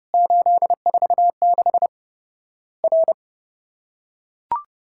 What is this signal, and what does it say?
Morse code, heard as a keyed tone, about 30 wpm, 700 Hz. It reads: 846 R
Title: Worked All Europe DX Contest - QTC Exchange - 30wpm